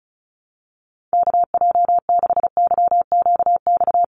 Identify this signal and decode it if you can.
Morse code, heard as a keyed tone, about 34 wpm, 700 Hz. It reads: KJ6YQX
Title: USA Call Signs Repeated 3x 35wpm